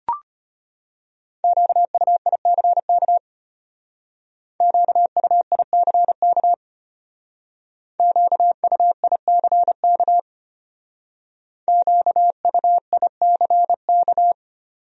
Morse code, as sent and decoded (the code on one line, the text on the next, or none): --.- ..- .. -.-. -.- / --.- ..- .. -.-. -.- / --.- ..- .. -.-. -.- / --.- ..- .. -.-. -.-
QUICK QUICK QUICK QUICK